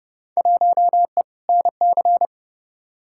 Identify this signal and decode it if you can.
Morse code, heard as a keyed tone, about 31 wpm, 700 Hz. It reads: 1E NC
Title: Field Day Exchanges - 30wpm